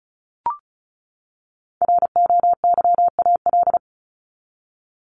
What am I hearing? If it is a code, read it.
Morse code, heard as a keyed tone, about 34 wpm, 700 Hz. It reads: ROYAL
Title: Top 1751-2000 Words 35wpm